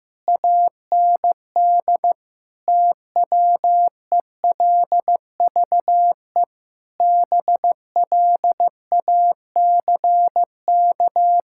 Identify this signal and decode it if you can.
Morse code, heard as a keyed tone, about 15 wpm, 700 Hz. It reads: AND TWELVE BLACK
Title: Pangram Challenge v2 - 15wpm